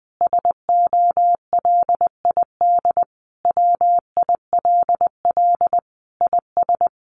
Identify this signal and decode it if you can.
Morse code, heard as a keyed tone, about 20 wpm, 700 Hz. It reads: SOLID WILL IS
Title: Sentences from Top 1500 Words 20wpm